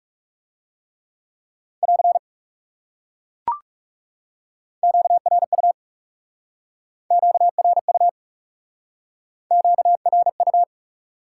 Morse code, heard as a keyed tone, about 40 wpm, 700 Hz. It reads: <AR> QRU QRU QRU